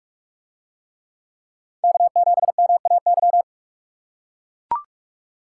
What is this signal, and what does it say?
Morse code, heard as a keyed tone, about 43 wpm, 700 Hz. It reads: K7MAY